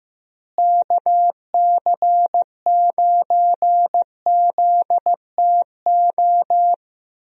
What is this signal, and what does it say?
Morse code, heard as a keyed tone, about 15 wpm, 700 Hz. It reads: KC9ZTO
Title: USA Call Signs - 6 Characters - Repeated 3x - 15wpm